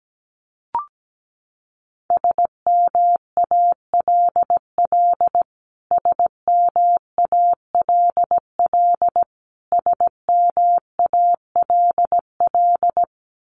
Morse code, heard as a keyed tone, about 17 wpm, 700 Hz. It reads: SMALL SMALL SMALL
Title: Top 500 Words Repeated 17wpm